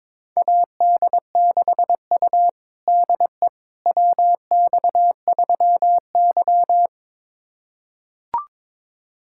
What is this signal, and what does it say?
Morse code, heard as a keyed tone, about 22 wpm, 700 Hz. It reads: AD6U DE WX3Y
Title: USA Callsign DE Callsign - 4 Characters - 22wpm